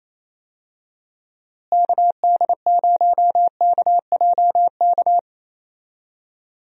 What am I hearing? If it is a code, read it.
Morse code, heard as a keyed tone, about 28 wpm, 700 Hz. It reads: KD0KJK
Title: USA Call Signs - 6 Characters - Repeated 3x - 28wpm